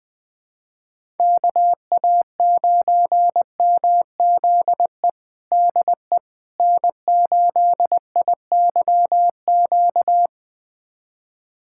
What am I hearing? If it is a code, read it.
Morse code, heard as a keyed tone, about 21 wpm, 700 Hz. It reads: KA9MZE DE N8IYQ
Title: USA Callsign DE Callsign 20wpm